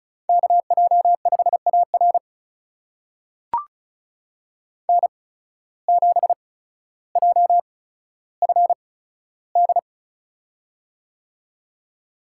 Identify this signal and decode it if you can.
Morse code, heard as a keyed tone, about 36 wpm, spaced as for 15 wpm, 700 Hz. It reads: KJ5AR N7JFD
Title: USA Call Signs - 5 Characters - 8x Character Spacing - 35wpm